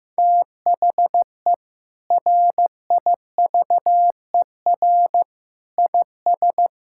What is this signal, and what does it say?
Morse code, heard as a keyed tone, about 15 wpm, 700 Hz. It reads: THE RIVER IS
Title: Sentences from Top 1000 Words - Review Part 6 - 15wpm